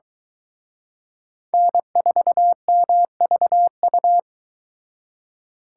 Morse code, heard as a keyed tone, about 22 wpm, 700 Hz. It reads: N4MVU